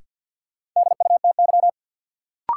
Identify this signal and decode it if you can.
Morse code, heard as a keyed tone, about 47 wpm, 700 Hz. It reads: DUTY
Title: Sentences from Top 1500 Words - New Word Drill - 50wpm